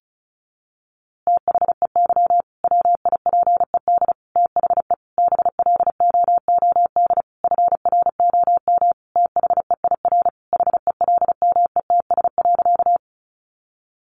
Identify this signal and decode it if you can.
Morse code, heard as a keyed tone, about 34 wpm, 700 Hz. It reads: THEY WIPED THE BLOOD FROM THEIR HELMETS.
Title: A Sound Of Thunder 35wpm